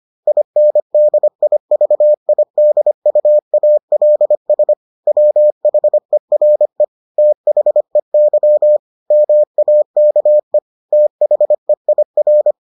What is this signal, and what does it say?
Morse code, heard as a keyed tone, about 25 wpm, 600 Hz. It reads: INDIVIDUALS WHERE THEY MAKE THEIR